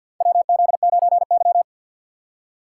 Morse code, heard as a keyed tone, about 50 wpm, 700 Hz. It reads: WZ9Y
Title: USA Call Signs - 4 Characters - 3x Character Spacing - 50wpm